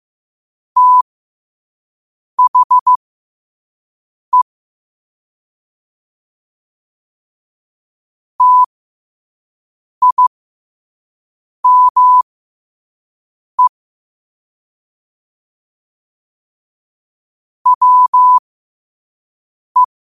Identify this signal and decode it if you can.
Morse code, heard as a keyed tone, about 15 wpm, spaced as for 5 wpm, 1000 Hz. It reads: THE TIME WE
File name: qotdfiles / 5wpm